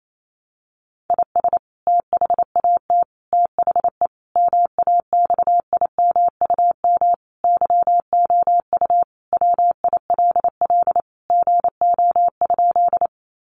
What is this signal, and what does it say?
Morse code, heard as a keyed tone, about 28 wpm, 700 Hz. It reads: IS THAT THE MAXIMUM YOU WILL GO?
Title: Sentences from Top 1300 Words 28wpm